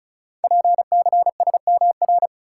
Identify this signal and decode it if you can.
Morse code, heard as a keyed tone, about 33 wpm, 700 Hz. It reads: PCSMR
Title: Code Groups - 5 Letter:Numbers - 35wpm